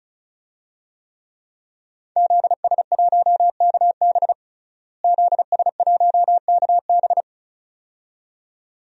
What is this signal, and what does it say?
Morse code, heard as a keyed tone, about 36 wpm, 700 Hz. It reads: ZS1KB ZS1KB
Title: DXCC-100 Call Signs - 5 Characters - Repeated 2x - 35wpm